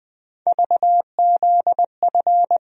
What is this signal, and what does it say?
Morse code, heard as a keyed tone, about 20 wpm, 700 Hz. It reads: VZF
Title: USA Call Sign Postfixes - 3 Letters 20wpm